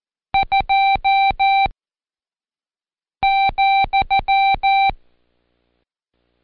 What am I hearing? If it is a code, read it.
Morse code, heard as a keyed tone, about 14 wpm, 780 Hz. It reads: 2 ,
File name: Code Quick / AudioDownld1 / Seg74-Day8-Pract 1234